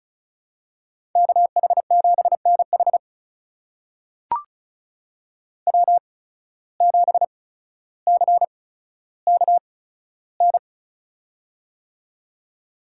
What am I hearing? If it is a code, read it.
Morse code, heard as a keyed tone, about 36 wpm, spaced as for 15 wpm, 700 Hz. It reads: KH7NH W7CKN
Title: USA Call Signs - 5 Characters - 8x Character Spacing - 35wpm